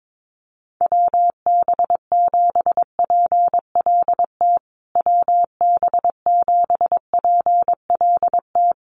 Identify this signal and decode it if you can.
Morse code, heard as a keyed tone, about 22 wpm, 700 Hz. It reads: WB7PLT WB7PLT
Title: USA Call Signs Repeated 3x 22wpm